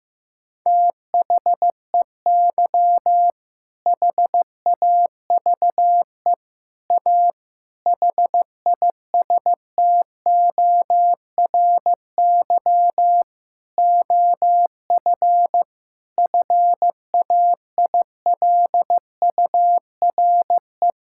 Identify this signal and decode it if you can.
Morse code, heard as a keyed tone, about 15 wpm, 700 Hz. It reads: THEY HAVE A HISTORY OF FAILURE
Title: Sentences from Top 1000 Words - Review Part 6 - 15wpm